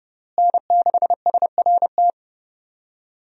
Morse code, heard as a keyed tone, about 32 wpm, 700 Hz. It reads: N6SRT
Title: USA Call Signs - 5 Characters - Repeated 2x - 30wpm